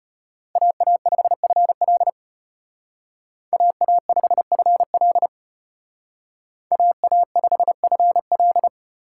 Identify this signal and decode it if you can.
Morse code, heard as a keyed tone, about 32 wpm, 700 Hz. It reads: AA5FL AA5FL AA5FL